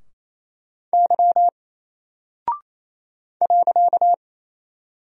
Morse code, heard as a keyed tone, about 28 wpm, 700 Hz. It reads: Y .
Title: Single Character (Letter:Number:Punctuation:Prosign) 28wpm